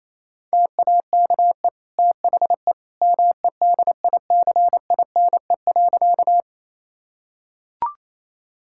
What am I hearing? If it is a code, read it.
Morse code, heard as a keyed tone, about 28 wpm, 700 Hz. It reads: TAKE THE MEDICINE.
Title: Sentences from Top 1000 Words - Review Part 6 - 28wpm